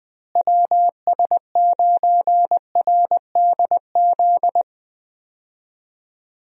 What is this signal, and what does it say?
Morse code, heard as a keyed tone, about 20 wpm, 700 Hz. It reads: WS9RDZ
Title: USA Call Signs - 6 Characters - Repeated 2x - 20wpm